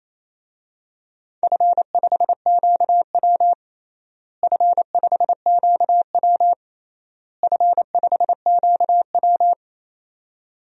Morse code, heard as a keyed tone, about 27 wpm, 700 Hz. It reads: F5QW F5QW F5QW
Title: DXCC-100 Call Signs - 4 Characters - Repeated 3x - 28wpm